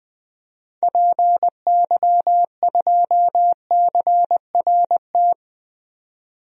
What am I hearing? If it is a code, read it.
Morse code, heard as a keyed tone, about 20 wpm, 700 Hz. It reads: PY2CRT
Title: DXCC-100 Call Signs - 6 Characters - 20wpm